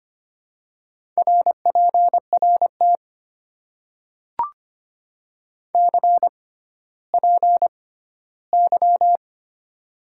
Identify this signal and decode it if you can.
Morse code, heard as a keyed tone, about 25 wpm, spaced as for 14 wpm, 700 Hz. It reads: RPRT CPY
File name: Top 90 QSO Elements - 6x Character Spacing - 25wpm